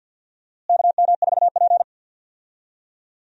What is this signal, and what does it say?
Morse code, heard as a keyed tone, about 47 wpm, 700 Hz. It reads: KM4P
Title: USA Call Signs - 4 Characters - Repeated 3x - 50wpm